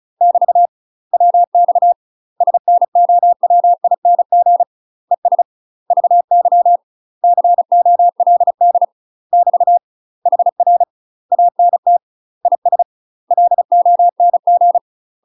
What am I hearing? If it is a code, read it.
Morse code, heard as a keyed tone, about 36 wpm, 700 Hz. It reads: <BT> WX SNOWING ES VY COLD <BT> HR ANT IS LONG
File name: UK MASTER QSOs TRAINING / 35WPM / UK MASTER QSOs 0041